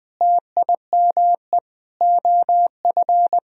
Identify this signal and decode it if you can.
Morse code, heard as a keyed tone, about 20 wpm, 700 Hz. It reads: TIME OF
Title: Sentences from Top 1000 Words - Review Part 4 - 20wpm